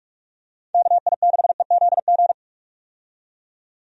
Morse code, heard as a keyed tone, about 46 wpm, 700 Hz. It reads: KI6EZG
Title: USA Call Signs - 6 Characters - 45wpm